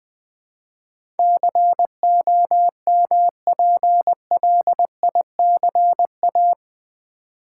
Morse code, heard as a keyed tone, about 20 wpm, 700 Hz. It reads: COMPLICA
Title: Build A Word - 4 Letters At A Time - 12-Letter Words - 20wpm